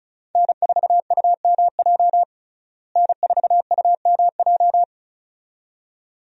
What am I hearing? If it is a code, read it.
Morse code, heard as a keyed tone, about 36 wpm, 700 Hz. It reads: N4UMJ N4UMJ
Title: USA Call Signs - 5 Characters - Repeated 2x - 35wpm